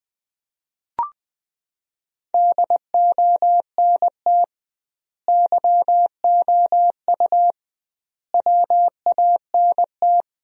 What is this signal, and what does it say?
Morse code, heard as a keyed tone, about 20 wpm, 700 Hz. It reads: DONT YOU WANT
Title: 2x Word-Spacing - Sentences from Top 300 Words - 20wpm